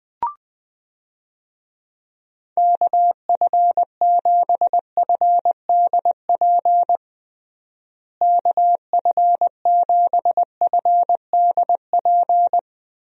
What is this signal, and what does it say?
Morse code, heard as a keyed tone, about 20 wpm, 700 Hz. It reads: KF7FDP KF7FDP
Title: USA Call Signs - 6 Characters - Repeated 3x - 20wpm